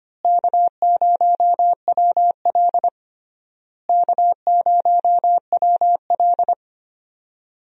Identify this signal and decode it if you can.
Morse code, heard as a keyed tone, about 25 wpm, 700 Hz. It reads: K0WL K0WL
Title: USA Call Signs - 4 Characters - Repeated 3x - 25wpm